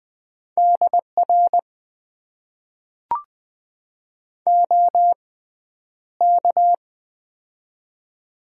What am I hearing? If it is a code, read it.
Morse code, heard as a keyed tone, about 20 wpm, spaced as for 10 wpm, 700 Hz. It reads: DR OK
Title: Top 90 QSO Elements - 6x Character Spacing - 20wpm